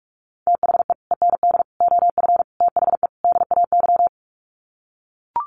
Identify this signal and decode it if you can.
Morse code, heard as a keyed tone, about 44 wpm, 700 Hz. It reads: THE END OF THE DAY
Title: Sets of 5 Words 45wpm